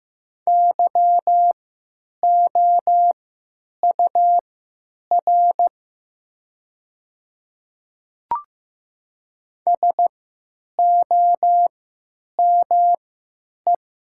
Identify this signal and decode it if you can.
Morse code, heard as a keyed tone, about 15 wpm, spaced as for 9 wpm, 700 Hz. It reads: YOUR SOME
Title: 3x Character Spacing - Top 100 Words - 15wpm